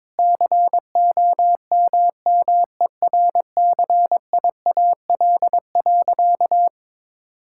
Morse code, heard as a keyed tone, about 22 wpm, 700 Hz. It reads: COMMERCIAL.
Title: Sentences from Top 1000 Words - Review Part 5 - 22wpm